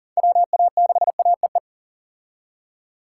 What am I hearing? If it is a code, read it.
Morse code, heard as a keyed tone, about 40 wpm, 700 Hz. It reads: WA6AEE
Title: K1USN SST QSO Practice - 40wpm